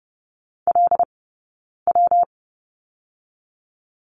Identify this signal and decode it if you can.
Morse code, heard as a keyed tone, about 30 wpm, 700 Hz. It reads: L W
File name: Lesson 19d - Letter w - Two Characters 30wpm